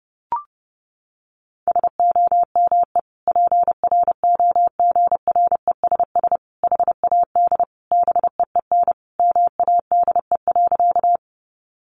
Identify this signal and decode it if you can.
Morse code, heard as a keyed tone, about 28 wpm, 700 Hz. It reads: SOME PROGRESS HAD BEEN MADE.
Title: Sentences from Top 1400 Words 30wpm